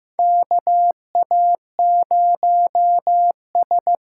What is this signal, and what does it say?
Morse code, heard as a keyed tone, about 15 wpm, 700 Hz. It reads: KA0S
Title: USA Call Signs - 6 Characters - 6x Character Spacing - 15wpm